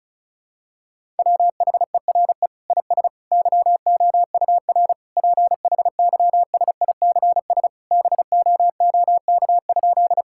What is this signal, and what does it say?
Morse code, heard as a keyed tone, about 36 wpm, 700 Hz. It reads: WHERE IS YOUR PHYSICS BOOK?
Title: Sentences from Top 1000 Words - Review Part 5 - 35wpm